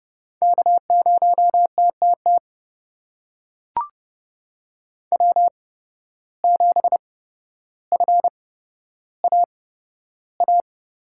Morse code, heard as a keyed tone, about 32 wpm, spaced as for 13 wpm, 700 Hz. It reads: K0TTT W7FAA
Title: USA Call Signs - 5 Characters - 8x Character Spacing - 30wpm